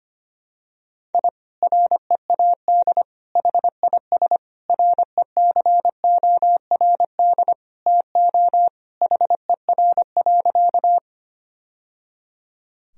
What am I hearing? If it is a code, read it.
Morse code, heard as a keyed tone, about 25 wpm, 700 Hz. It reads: I READ HIS RECORD TO HER.
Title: Sentences from Top 1000 Words - Review Part 3 - 25wpm